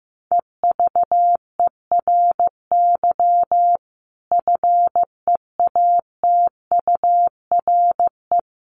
Morse code, heard as a keyed tone, about 15 wpm, 700 Hz. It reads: EVERY FEATURE
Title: Sentences from Top 800 Words 15wpm